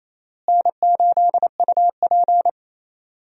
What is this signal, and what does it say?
Morse code, heard as a keyed tone, about 28 wpm, 700 Hz. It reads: N8UP
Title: USA Call Signs - 4 Characters - Repeated 3x - 28wpm